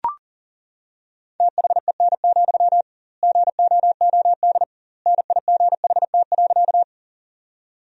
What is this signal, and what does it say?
Morse code, heard as a keyed tone, about 40 wpm, 700 Hz. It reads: THEN, GOOD NIGHT.
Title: 2x Word-Spacing - Sentences from Top 300 Words - 40wpm